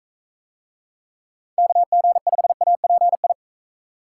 Morse code, heard as a keyed tone, about 42 wpm, 700 Hz. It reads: KG5API